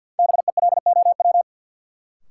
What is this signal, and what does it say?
Morse code, heard as a keyed tone, about 53 wpm, 700 Hz. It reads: BELOW
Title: Sentences from Top 1200 Words - New Word Drill Alt - 50wpm